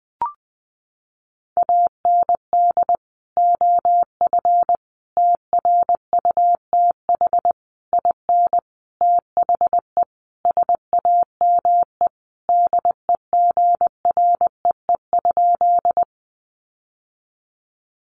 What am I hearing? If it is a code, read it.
Morse code, heard as a keyed tone, about 20 wpm, 700 Hz. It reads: AND OF TRUTH IN THE SAME DEGREE?
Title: Sentences from Top 1500 Words 20wpm